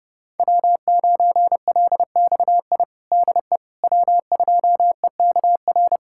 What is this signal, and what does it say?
Morse code, heard as a keyed tone, about 29 wpm, 700 Hz. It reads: W9LXI DE W2EKR
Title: USA Callsign DE Callsign - 5 Characters - 30wpm